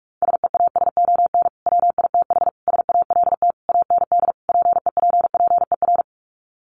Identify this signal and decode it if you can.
Morse code, heard as a keyed tone, about 44 wpm, 700 Hz. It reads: SEASON WITH SALT AND PEPPER
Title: Sets of 5 Words 45wpm